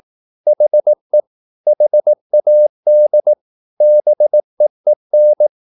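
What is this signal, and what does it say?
Morse code, heard as a keyed tone, about 18 wpm, 600 Hz. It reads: HE HAD BEEN
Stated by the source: Morse practice at 18 wpm, English text, from CW med Gnister.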